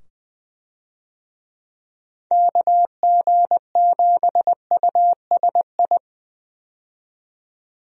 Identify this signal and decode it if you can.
Morse code, heard as a keyed tone, about 21 wpm, 700 Hz. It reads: KG7USI
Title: USA Call Signs - 6 Characters - 20wpm